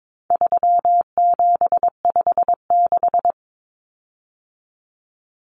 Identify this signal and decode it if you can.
Morse code, heard as a keyed tone, about 22 wpm, 700 Hz. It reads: 3756
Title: Common Cut Numbers - 4-Digit 22wpm